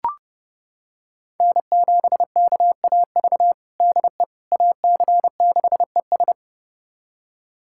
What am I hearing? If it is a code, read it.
Morse code, heard as a keyed tone, about 32 wpm, 700 Hz. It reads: N7KAV DE AC6ES
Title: USA Callsign DE Callsign - 5 Characters - 30wpm